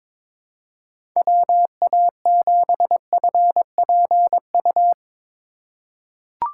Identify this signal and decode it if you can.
Morse code, heard as a keyed tone, about 22 wpm, 700 Hz. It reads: WA7FPU
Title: USA Call Signs - 6 Characters - 3x Character Spacing - 22wpm